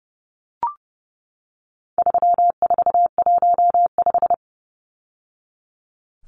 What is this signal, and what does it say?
Morse code, heard as a keyed tone, about 30 wpm, 700 Hz. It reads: 3415
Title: Numbers - 4-Digit 30wpm